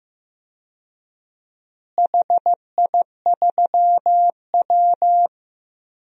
Morse code, heard as a keyed tone, about 15 wpm, 700 Hz. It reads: HI3W
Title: DXCC-100 Call Signs - 4 Characters - Repeated 2x - 15wpm